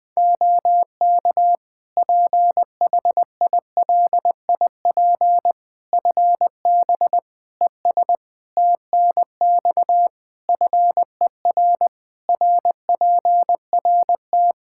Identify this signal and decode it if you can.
Morse code, heard as a keyed tone, about 21 wpm, 700 Hz. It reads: OK PHILIP FB ES TNX FER RPRT
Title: K1USN SST QSO Practice 20wpm